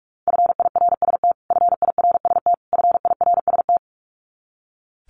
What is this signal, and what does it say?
Morse code, heard as a keyed tone, about 44 wpm, 700 Hz. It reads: FIRST FIRST FIRST
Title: Top 500 Words Repeated 45wpm